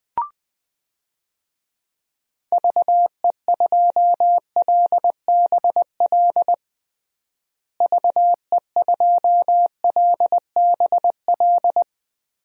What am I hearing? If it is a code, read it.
Morse code, heard as a keyed tone, about 21 wpm, 700 Hz. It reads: VE2LBL VE2LBL
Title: DXCC-100 Call Signs - 6 Characters - Repeated 3x - 20wpm